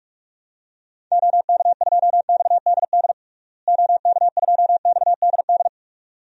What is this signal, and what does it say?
Morse code, heard as a keyed tone, about 46 wpm, 700 Hz. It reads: OK2XDD OK2XDD
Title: DXCC-100 Call Signs - 6 Characters - Repeated 2x - 45wpm